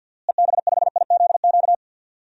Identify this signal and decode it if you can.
Morse code, heard as a keyed tone, about 46 wpm, 700 Hz. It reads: EB5IZQ
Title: DXCC-100 Call Signs - 6 Characters - Repeated 3x - 50wpm